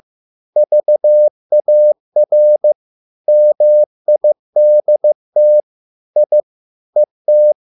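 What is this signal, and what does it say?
Morse code, heard as a keyed tone, about 15 wpm, 600 Hz. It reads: VAR MIDT I ET
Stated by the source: Morse code at a slow 15 wpm from CW med Gnister.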